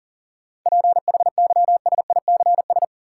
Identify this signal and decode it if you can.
Morse code, heard as a keyed tone, about 39 wpm, 700 Hz. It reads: PHYSICS
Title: Sentences from Top 1500 Words - New Word Drill Alt - 40wpm